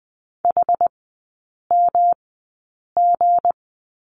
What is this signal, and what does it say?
Morse code, heard as a keyed tone, about 20 wpm, spaced as for 8 wpm, 700 Hz. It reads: HMG
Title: 159 Lesson 26e - Letter g - Three Characters 20wpm